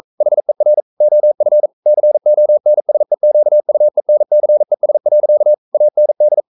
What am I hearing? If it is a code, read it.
Morse code, heard as a keyed tone, about 42 wpm, 600 Hz. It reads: HER OF CONSEQUENCES. AND
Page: Romanen Never af Ken Follet på engelsk.